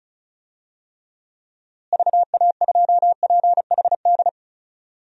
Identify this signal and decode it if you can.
Morse code, heard as a keyed tone, about 34 wpm, 700 Hz. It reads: VA2PHD
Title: DXCC-100 Call Signs - 6 Characters - Repeated 2x - 35wpm